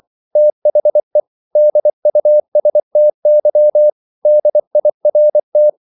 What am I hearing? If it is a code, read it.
Morse code, heard as a keyed tone, about 24 wpm, 600 Hz. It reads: THE DUSTY DIRT